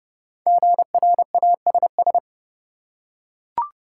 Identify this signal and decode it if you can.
Morse code, heard as a keyed tone, about 32 wpm, 700 Hz. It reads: GRASS
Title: Five Letter Words - 30wpm